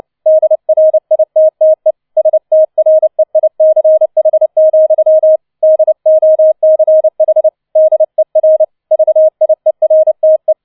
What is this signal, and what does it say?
Morse code, heard as a keyed tone, about 29 wpm, 630 Hz. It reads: DRITTE STREICH, DOCH DER VIERTE